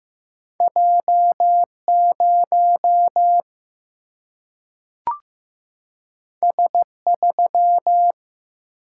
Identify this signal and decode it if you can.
Morse code, heard as a keyed tone, about 15 wpm, 700 Hz. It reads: J 0 S 3
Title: Single-Letter Single-Number 15wpm